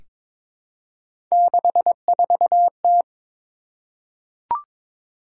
Morse code, heard as a keyed tone, about 22 wpm, 700 Hz. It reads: 64T
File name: practice / 22wpm / Three Letter-Number Combinations - 22wpm